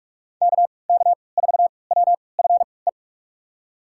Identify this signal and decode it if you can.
Morse code, heard as a keyed tone, about 45 wpm, spaced as for 25 wpm, 700 Hz. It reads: KK4WFE
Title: USA Call Signs - 3x Character Spacing - 45wpm